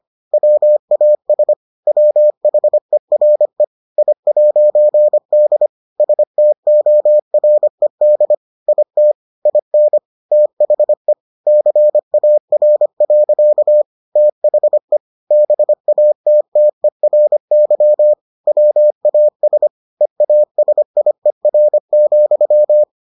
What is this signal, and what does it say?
Morse code, heard as a keyed tone, about 25 wpm, 600 Hz. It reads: WAS WHERE I'D STORED IT IN THE CAR. THE BATTERY WAS EASIER,